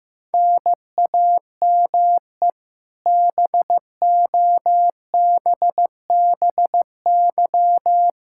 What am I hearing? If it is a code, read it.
Morse code, heard as a keyed tone, about 15 wpm, 700 Hz. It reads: NAME BOBBY
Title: Practice QSO - Exchange 1 - Beginning - 15wpm